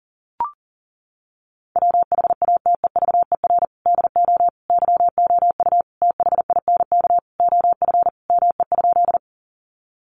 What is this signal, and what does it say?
Morse code, heard as a keyed tone, about 40 wpm, 700 Hz. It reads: WHATEVER DO YOU THINK OF ME?
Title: Sentences from Top 1100 Words 40wpm